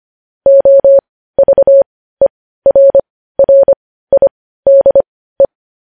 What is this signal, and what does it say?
Morse code, heard as a keyed tone, about 25 wpm, spaced as for 15 wpm, 550 Hz. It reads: OVERRIDE
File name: podcast / News headlines-1x-25x15